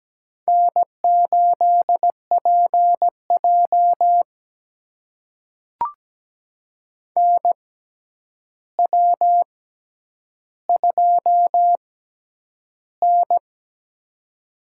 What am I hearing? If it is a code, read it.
Morse code, heard as a keyed tone, about 17 wpm, spaced as for 9 wpm, 700 Hz. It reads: N8PJ NW2N
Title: USA Call Signs - 4 Characters - 6x Character Spacing - 17wpm